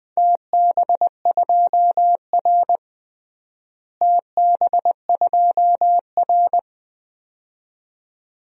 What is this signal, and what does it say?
Morse code, heard as a keyed tone, about 20 wpm, 700 Hz. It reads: TB2R TB2R
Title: DXCC-100 Call Signs - 4 Characters - Repeated 2x - 20wpm